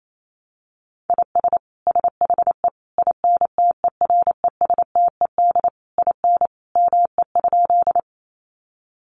Morse code, heard as a keyed tone, about 28 wpm, 690 Hz. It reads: IS SHE INTERESTED IN ME?
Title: Sentences from Top 900 Words 28wpm